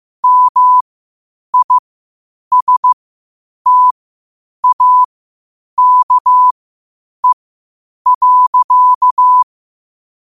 Morse code, heard as a keyed tone, about 15 wpm, spaced as for 8 wpm, 1000 Hz. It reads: MISTAKE.
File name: qotdfiles / 8wpm